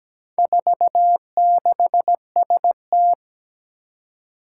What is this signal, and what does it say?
Morse code, heard as a keyed tone, about 17 wpm, 700 Hz. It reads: 46ST
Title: Four Challenging Letter-Number Combinations 17wpm